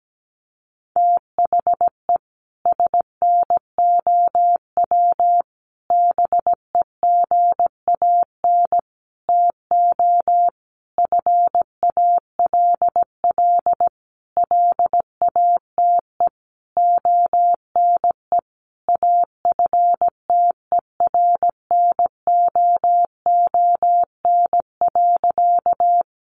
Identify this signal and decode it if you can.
Morse code, heard as a keyed tone, about 17 wpm, 700 Hz. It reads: THE SNOW BEGAN TO FALL LATE ONE AFTERNOON.
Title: Sentences from Top 1500 Words 17wpm